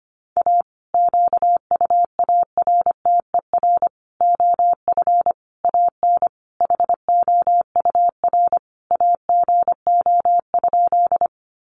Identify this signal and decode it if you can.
Morse code, heard as a keyed tone, about 25 wpm, 700 Hz. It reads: A QUARTER OF AN HOUR AGO?
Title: Sentences from Top 1300 Words 25wpm